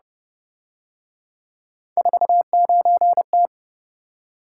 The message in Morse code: 49T

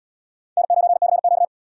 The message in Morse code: N9CQ

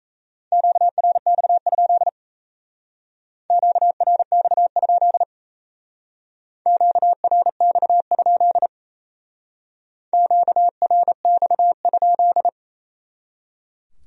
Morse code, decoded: QRX? QRX? QRX? QRX?

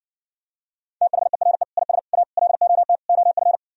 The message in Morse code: THERE IS A LOT OF

